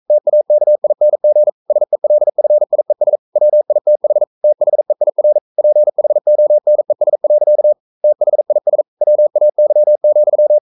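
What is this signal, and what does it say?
Morse code, heard as a keyed tone, about 42 wpm, 600 Hz. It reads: TAKING SELFIES WITH THEIR PHONES. THIS WAY,
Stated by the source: Morse code practice at 41 wpm, from CW med Gnister.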